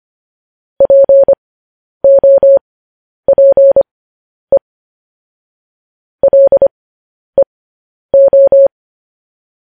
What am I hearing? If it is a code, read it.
Morse code, heard as a keyed tone, about 25 wpm, spaced as for 10 wpm, 550 Hz. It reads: POPE LEO
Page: News Headlines in Morse Code at 10 WPM